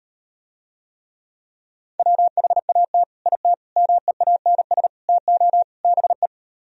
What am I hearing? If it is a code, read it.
Morse code, heard as a keyed tone, about 38 wpm, 700 Hz. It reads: WHAT IT MEANS TO BE